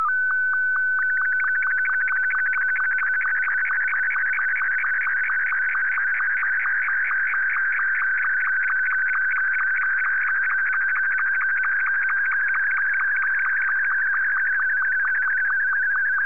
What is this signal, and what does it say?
SSTV (slow-scan television), Martin 2.
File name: sstv / M2-VK3HJV